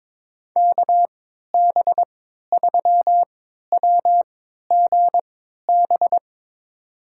Morse code, heard as a keyed tone, about 22 wpm, spaced as for 12 wpm, 700 Hz. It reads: KB3WGB
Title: USA Call Signs - 3x Character Spacing - 22wpm